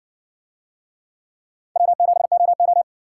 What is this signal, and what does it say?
Morse code, heard as a keyed tone, about 60 wpm, 700 Hz. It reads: W7OO